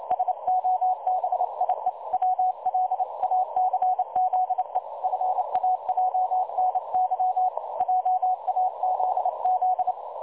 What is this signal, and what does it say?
Morse code, heard as a keyed tone, about 30 wpm, 740 Hz. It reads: SOBIE WLACZE FAJNY JAZZ
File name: dlazaawansowanych / nagraniazpasma / websdr recording 3565.0kHz